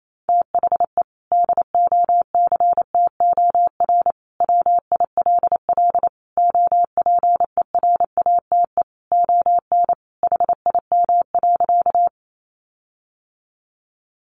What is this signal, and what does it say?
Morse code, heard as a keyed tone, about 28 wpm, 700 Hz. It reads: THE DOCTOR WILL OPERATE ON HIM.
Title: Sentences from Top 1500 Words 28wpm